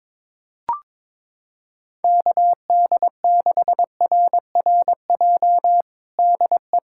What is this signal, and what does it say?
Morse code, heard as a keyed tone, about 22 wpm, 700 Hz. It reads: KD6RRJ DE
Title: USA Callsign DE Callsign - 6 Characters - 22wpm